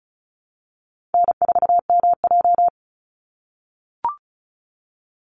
Morse code, heard as a keyed tone, about 34 wpm, 700 Hz. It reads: N4MJ
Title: NA SOTA Chaser Callsigns 35wpm